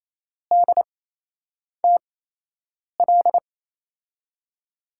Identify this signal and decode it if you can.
Morse code, heard as a keyed tone, about 28 wpm, spaced as for 8 wpm, 700 Hz. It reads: DTL
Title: USA Call Signs - 6 Characters - 8x Character Spacing - 28wpm